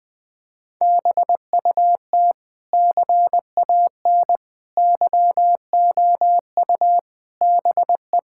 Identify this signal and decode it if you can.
Morse code, heard as a keyed tone, about 20 wpm, 700 Hz. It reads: BUT CAN YOU BE
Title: Sentences from Top 1000 Words - Review Part 6 - 20wpm